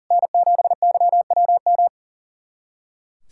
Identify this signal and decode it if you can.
Morse code, heard as a keyed tone, about 39 wpm, 700 Hz. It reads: N7YWM
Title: SOTA QSO Practice 40wpm